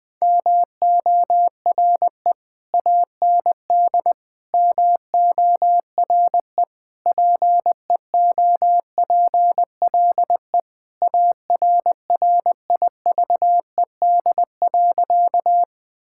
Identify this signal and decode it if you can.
Morse code, heard as a keyed tone, about 20 wpm, 700 Hz. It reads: MORE AND MORE PEOPLE ARRIVED.
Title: Sentences with 3-Word Phrases - Spoken Beginning - 20wpm